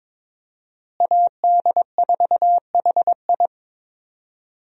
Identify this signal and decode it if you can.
Morse code, heard as a keyed tone, about 22 wpm, 700 Hz. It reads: AD4HI
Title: USA Call Signs - 5 Characters - Repeated 3x - 22wpm